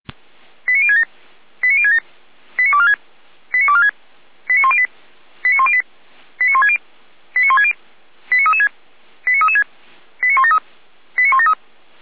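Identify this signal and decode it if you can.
Five-tone selective call (ZVEI), heard as quick runs of five tones, tones of about 70 ms. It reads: 89067 89067 89246 89246 89108 89108 89160 89160 89336 89336 89173 89173
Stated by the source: ZVEI five-tone call received by SDR.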